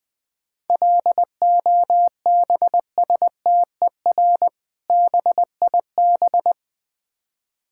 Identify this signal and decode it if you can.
Morse code, heard as a keyed tone, about 20 wpm, 700 Hz. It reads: LOBSTER BIB
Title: Difficult Words - Speed-Racing with Hints - 20wpm